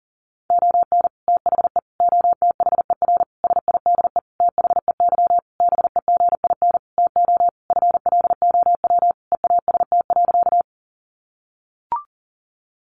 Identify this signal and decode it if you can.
Morse code, heard as a keyed tone, about 40 wpm, 700 Hz. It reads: ON THE OTHER SIDE THEY BEGIN TO FLOW EAST.
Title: Sentences from Top 1200 Words 40wpm